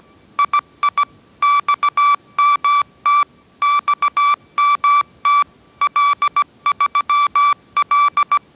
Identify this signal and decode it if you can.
Morse code, heard as a keyed tone, about 17 wpm, 1160 Hz. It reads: II XMT XMT L3L